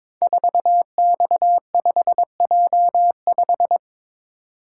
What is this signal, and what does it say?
Morse code, heard as a keyed tone, about 22 wpm, 700 Hz. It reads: 4X5J5
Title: Code Groups - 5 Letter:Numbers - 22wpm